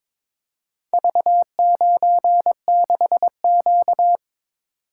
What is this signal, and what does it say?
Morse code, heard as a keyed tone, about 22 wpm, 700 Hz. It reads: V96Q
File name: practice / 22wpm / Four Letter-Number Combinations - 22wpm